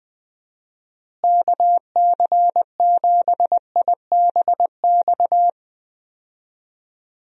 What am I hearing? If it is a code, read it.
Morse code, heard as a keyed tone, about 20 wpm, 700 Hz. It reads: KC7IBX
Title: USA Call Signs - 6 Characters - Repeated 3x - 20wpm